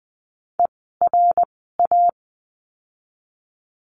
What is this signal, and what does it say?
Morse code, heard as a keyed tone, about 20 wpm, 700 Hz. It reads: E R A
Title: 054 Lesson 10f - Letter r - Words 20wpm